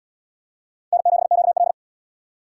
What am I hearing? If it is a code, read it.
Morse code, heard as a keyed tone, about 65 wpm, 700 Hz. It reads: N7QL